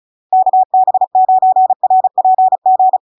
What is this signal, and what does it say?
Morse code, heard as a keyed tone, about 35 wpm, 750 Hz. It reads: KB9RPG
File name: LARK-FD-2022-Log-in-CW-35wpm-750-Hz